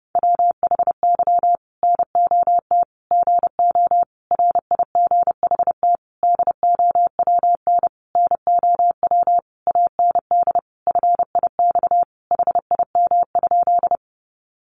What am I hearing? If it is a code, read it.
Morse code, heard as a keyed tone, about 30 wpm, 700 Hz. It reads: WHY NOT GO RIGHT DOWN NOW AND FIX HIM?
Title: Sentences from Top 1400 Words 30wpm